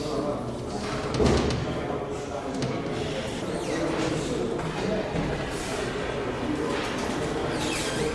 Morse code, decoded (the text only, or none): none